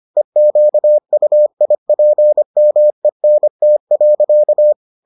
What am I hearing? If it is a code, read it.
Morse code, heard as a keyed tone, about 25 wpm, 600 Hz. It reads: EQUIPMENT.